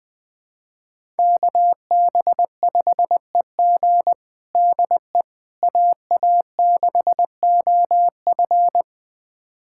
Morse code, heard as a keyed tone, about 20 wpm, 700 Hz. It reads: KB5EG DE AA6OF
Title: USA Callsign DE Callsign - 5 Characters - 20wpm